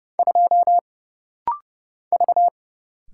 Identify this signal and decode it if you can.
Morse code, heard as a keyed tone, about 31 wpm, 700 Hz. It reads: 2 V
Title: Single Letter-Number Alt - Rapid-Fire 30wpm